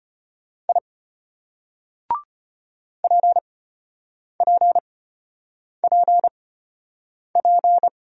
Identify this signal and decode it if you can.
Morse code, heard as a keyed tone, about 33 wpm, spaced as for 8 wpm, 700 Hz. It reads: I PPPP